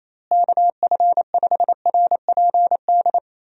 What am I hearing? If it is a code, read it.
Morse code, heard as a keyed tone, about 28 wpm, 700 Hz. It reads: KF5RPD
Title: USA Call Signs - 6 Characters - Repeated 3x - 28wpm